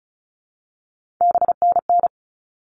Morse code, heard as a keyed tone, about 34 wpm, 700 Hz. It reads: BNN
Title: All Cut Numbers - 3-Digit 35wpm